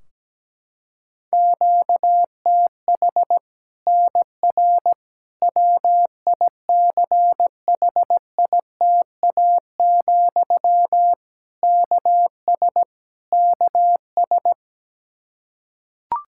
Morse code, heard as a keyed tone, about 17 wpm, 700 Hz. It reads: QTH NR WICHITA, KS KS
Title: Practice QSO - Exchange 1 - Beginning - 17wpm